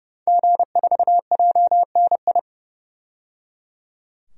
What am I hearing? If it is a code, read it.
Morse code, heard as a keyed tone, about 30 wpm, 690 Hz. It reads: G4JNI
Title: DXCC-100 Call Signs - 5 Characters - 30wpm